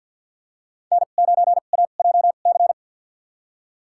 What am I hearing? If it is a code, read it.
Morse code, heard as a keyed tone, about 50 wpm, spaced as for 36 wpm, 700 Hz. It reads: N9AJC
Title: USA Call Signs - 2x Character Spacing - 50wpm